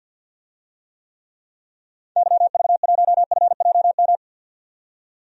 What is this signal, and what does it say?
Morse code, heard as a keyed tone, about 47 wpm, 700 Hz. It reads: YV1FJM
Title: DXCC-100 Call Signs - 6 Characters - 50wpm